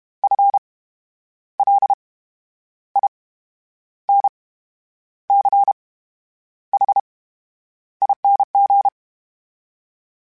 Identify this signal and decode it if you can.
Morse code, heard as a keyed tone, about 32 wpm, spaced as for 10 wpm, 800 Hz. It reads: FLINCHING
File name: Special INGs / Special INGs-mp3s / VerySlowToFaster Ing @30